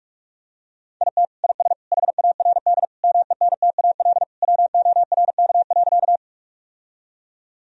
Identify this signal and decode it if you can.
Morse code, heard as a keyed tone, about 46 wpm, 700 Hz. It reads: IT IS HARD MENTAL WORK.
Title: Sentences from Top 1000 Words - Review Part 4 - 45wpm